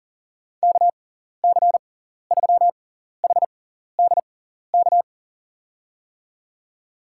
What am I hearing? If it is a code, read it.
Morse code, heard as a keyed tone, about 40 wpm, spaced as for 14 wpm, 700 Hz. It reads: KC3HDK
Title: USA Call Signs - 6 Characters - 6x Character Spacing - 40wpm